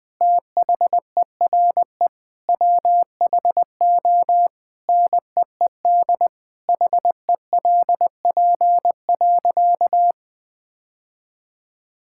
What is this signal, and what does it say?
Morse code, heard as a keyed tone, about 21 wpm, 700 Hz. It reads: THERE WHO NEED HELP.
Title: Sentences with 8-Word Phrases - 20wpm